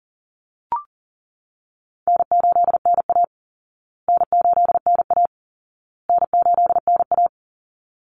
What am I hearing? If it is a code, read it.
Morse code, heard as a keyed tone, about 40 wpm, 700 Hz. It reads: N8NA N8NA N8NA
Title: Easy Call Signs Repeated 3x - 40wpm